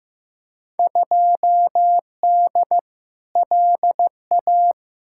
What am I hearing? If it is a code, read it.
Morse code, heard as a keyed tone, about 15 wpm, 700 Hz. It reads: 2D LA
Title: Field Day Exchanges - 15wpm